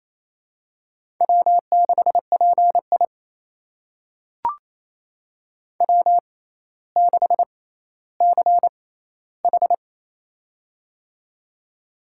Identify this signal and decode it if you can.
Morse code, heard as a keyed tone, about 28 wpm, spaced as for 14 wpm, 700 Hz. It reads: W6PI W6CH
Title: USA Call Signs - 4 Characters - 6x Character Spacing - 28wpm